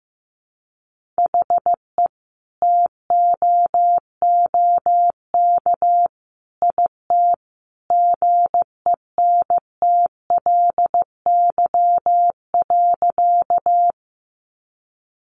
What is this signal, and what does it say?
Morse code, heard as a keyed tone, about 15 wpm, 700 Hz. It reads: HE TOOK IT GENTLY.
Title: Sentences from Top 1500 Words 15wpm